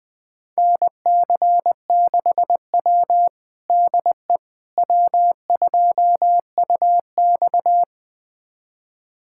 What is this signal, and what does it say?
Morse code, heard as a keyed tone, about 20 wpm, 700 Hz. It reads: NC6W DE W2UX
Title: USA Callsign DE Callsign - 4 Characters - 20wpm